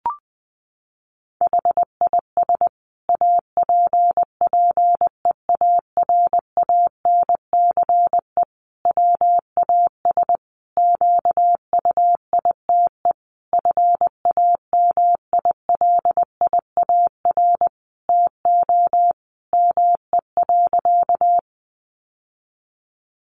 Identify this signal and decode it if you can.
Morse code, heard as a keyed tone, about 20 wpm, 700 Hz. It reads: HIS APPEARANCE WAS QUITE FAMILIAR TO ME.
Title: Sentences from Top 1100 Words 20wpm